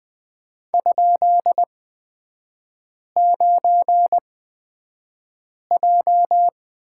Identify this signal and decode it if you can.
Morse code, heard as a keyed tone, about 20 wpm, spaced as for 5 wpm, 700 Hz. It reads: ?9J
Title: 224 Lesson 36e - BT - Three Characters 20wpm